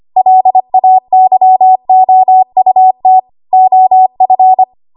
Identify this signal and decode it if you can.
Morse code, heard as a keyed tone, about 25 wpm, 750 Hz. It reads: LAYOUT OF
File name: Archive / 25 WPM / 25WPM